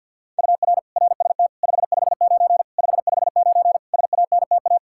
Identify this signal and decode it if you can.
Morse code, heard as a keyed tone, about 50 wpm, 700 Hz. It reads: UR RST 559 559 SANTA
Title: Practice QSO - Exchange 1 - Beginning - 50wpm